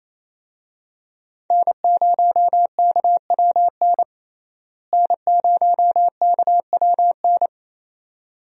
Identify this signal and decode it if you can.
Morse code, heard as a keyed tone, about 27 wpm, 700 Hz. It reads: N0KWN N0KWN
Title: USA Call Signs - 5 Characters - Repeated 2x - 28wpm